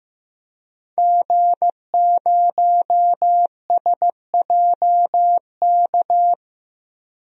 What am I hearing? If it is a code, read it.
Morse code, heard as a keyed tone, about 15 wpm, 700 Hz. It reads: G0SJK
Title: DXCC-100 Call Signs - 5 Characters - 15wpm